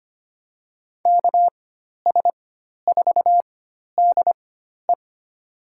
Morse code, heard as a keyed tone, about 25 wpm, spaced as for 12 wpm, 700 Hz. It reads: KS4DE
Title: USA Call Signs - 4x Character Spacing - 25wpm